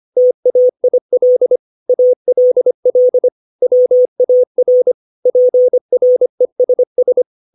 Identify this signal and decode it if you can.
Morse code, heard as a keyed tone, about 25 wpm, 500 Hz. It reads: TAIL ALL WAR PRESS
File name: 632 words / 632 Words 25WPM